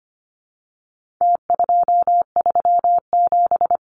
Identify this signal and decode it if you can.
Morse code, heard as a keyed tone, about 25 wpm, 700 Hz. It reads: T237
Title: Numbers - Contest Style - 4-Digit 25wpm